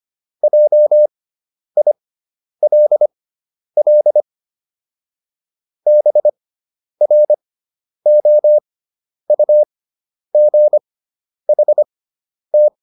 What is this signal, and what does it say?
Morse code, heard as a keyed tone, about 25 wpm, spaced as for 10 wpm, 600 Hz. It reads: JILL BROUGHT